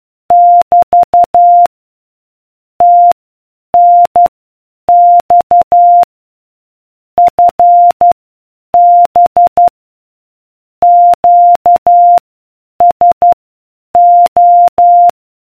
Morse code, heard as a keyed tone, about 12 wpm, spaced as for 8 wpm, 700 Hz. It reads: <BT> TNX FB QSO